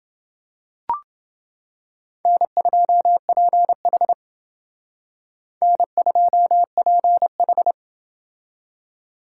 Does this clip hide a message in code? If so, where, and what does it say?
Yes: Morse code from 2.3 to 7.7 s, heard as a keyed tone, about 28 wpm, 700 Hz. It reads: N2PH N2PH